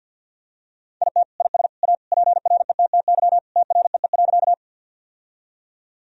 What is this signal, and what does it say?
Morse code, heard as a keyed tone, about 50 wpm, 700 Hz. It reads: IT IS A PRETTY TREE.
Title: Sentences from Top 1000 Words - Review Part 1 - 50wpm